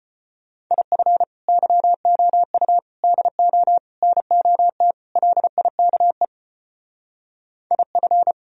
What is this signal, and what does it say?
Morse code, heard as a keyed tone, about 32 wpm, 700 Hz. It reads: IF YOU DO NOT LIKE IF